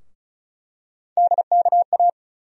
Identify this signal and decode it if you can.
Morse code, heard as a keyed tone, about 34 wpm, 700 Hz. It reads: DKA
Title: USA Call Sign Postfixes - 3 Letters 35wpm